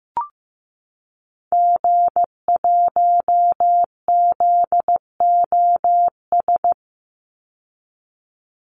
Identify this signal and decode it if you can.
Morse code, heard as a keyed tone, about 15 wpm, 700 Hz. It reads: G1ZOS
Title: Call Signs Repeated 2x - 15wpm